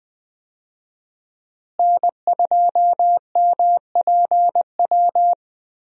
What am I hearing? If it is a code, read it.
Morse code, heard as a keyed tone, about 20 wpm, 700 Hz. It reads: N2MPW
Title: USA Call Signs - 5 Characters - Repeated 3x - 20wpm